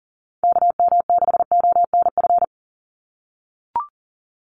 Morse code, heard as a keyed tone, about 39 wpm, 700 Hz. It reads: KM6ONF
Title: USA Call Signs 40wpm